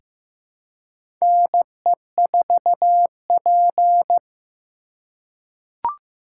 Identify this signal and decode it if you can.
Morse code, heard as a keyed tone, about 15 wpm, 700 Hz. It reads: NE4P
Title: USA Call Signs - 4 Characters - 4x Character Spacing - 15wpm